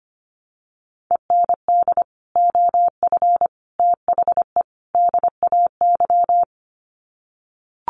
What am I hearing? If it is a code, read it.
Morse code, heard as a keyed tone, about 25 wpm, 700 Hz. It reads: END OF THE DAY
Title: Sets of 4 Words 25wpm